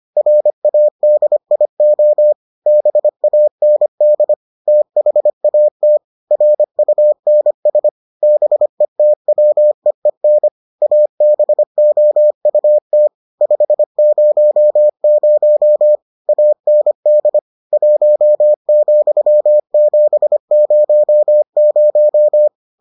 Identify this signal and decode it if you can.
Morse code, heard as a keyed tone, about 25 wpm, 600 Hz. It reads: RADIO BAND THAT RUNS BETWEEN ABOUT 500 AND 1,700